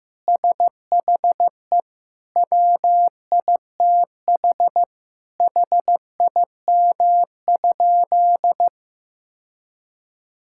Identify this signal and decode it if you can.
Morse code, heard as a keyed tone, about 15 wpm, 700 Hz. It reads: SHE WITH HIM?
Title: Sentences from Top 1000 Words - Review Part 5 - 15wpm